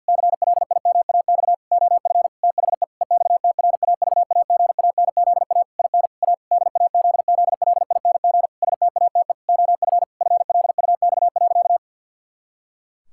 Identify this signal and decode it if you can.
Morse code, heard as a keyed tone, about 49 wpm, 700 Hz. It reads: CLIMAX OF THE EXTRAVAGANZA IN A DAZZLING STATE OF FLUX.
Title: Pangram Challenge v2 - 50wpm